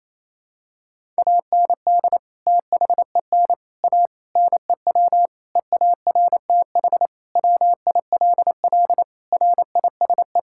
Morse code, heard as a keyed tone, about 28 wpm, 700 Hz. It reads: AND THEN A NEW EARTH WILL RISE
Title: Sentences from Top 1000 Words - Review Part 6 - 28wpm